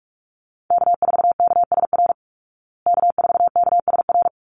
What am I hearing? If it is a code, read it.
Morse code, heard as a keyed tone, about 44 wpm, 700 Hz. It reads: K4KSR K4KSR
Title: Call Signs Repeated 3x - 45wpm